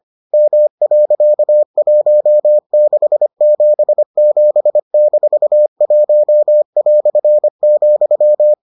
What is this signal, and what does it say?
Morse code, heard as a keyed tone, about 25 wpm, 600 Hz. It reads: M.1677-1",